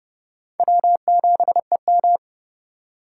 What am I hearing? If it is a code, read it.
Morse code, heard as a keyed tone, about 32 wpm, 700 Hz. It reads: W7EM
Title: USA Call Signs - 4 Characters - 3x Character Spacing - 30wpm